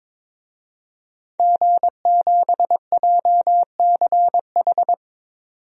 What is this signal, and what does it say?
Morse code, heard as a keyed tone, about 22 wpm, 700 Hz. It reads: G7JCH
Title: DXCC-100 Call Signs - 5 Characters - 22wpm